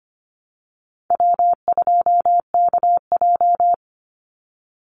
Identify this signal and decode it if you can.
Morse code, heard as a keyed tone, about 25 wpm, 700 Hz. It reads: W2KJ
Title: Call Signs Repeated 3x - 25wpm